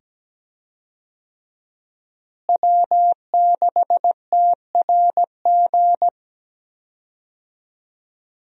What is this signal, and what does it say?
Morse code, heard as a keyed tone, about 17 wpm, 700 Hz. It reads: W6TRG
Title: USA Call Signs - 5 Characters - Repeated 2x - 17wpm